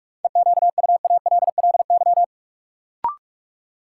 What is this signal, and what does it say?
Morse code, heard as a keyed tone, about 46 wpm, 700 Hz. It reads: EQUALLY
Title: Sentences from Top 1400 Words - New Word Drill - 45wpm